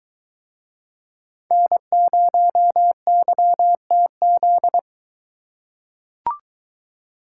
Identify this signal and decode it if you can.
Morse code, heard as a keyed tone, about 21 wpm, 700 Hz. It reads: N0YTZ